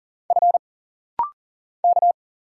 Morse code, heard as a keyed tone, about 40 wpm, 700 Hz. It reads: F K
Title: Single Letters - Rapid-Fire 40wpm